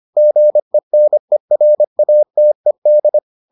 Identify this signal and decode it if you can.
Morse code, heard as a keyed tone, about 25 wpm, 600 Hz. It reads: GENERATED